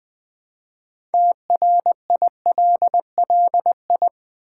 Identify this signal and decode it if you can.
Morse code, heard as a keyed tone, about 20 wpm, 700 Hz. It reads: TRILLI
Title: Build A Word - 3 Letters At A Time - 8-Letter Words - 20wpm